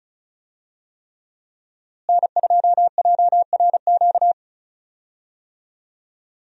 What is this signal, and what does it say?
Morse code, heard as a keyed tone, about 36 wpm, 700 Hz. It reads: N2JRQ
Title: USA Call Signs - 5 Characters - Repeated 3x - 35wpm